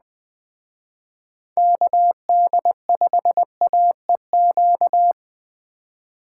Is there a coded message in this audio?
Yes: Morse code from 1.6 to 5.1 s, heard as a keyed tone, about 20 wpm, 700 Hz. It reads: KD5AEQ